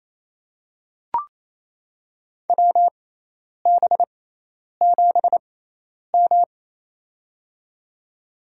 Morse code, heard as a keyed tone, about 28 wpm, spaced as for 10 wpm, 700 Hz. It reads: WB7M
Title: USA Call Signs - 4 Characters - 6x Character Spacing - 28wpm